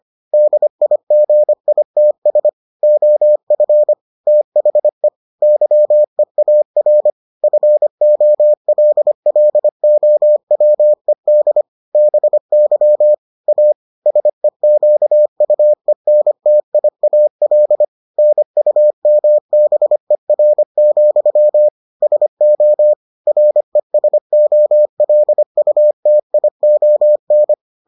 Morse code, heard as a keyed tone, about 25 wpm, 600 Hz. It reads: DIGITS OF THE YEAR FOLLOWED BY A SEQUENTIAL NUMBER, SO RESOLUTION